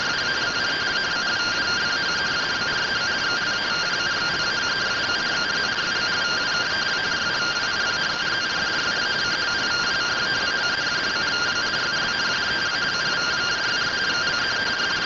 Two-tone teleprinter signal, about 170 Hz shift.